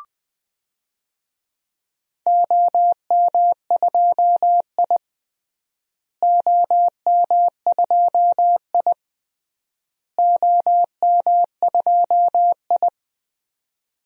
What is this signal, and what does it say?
Morse code, heard as a keyed tone, about 20 wpm, 700 Hz. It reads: OM2I OM2I OM2I
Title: DXCC-100 Call Signs - 4 Characters - Repeated 3x - 20wpm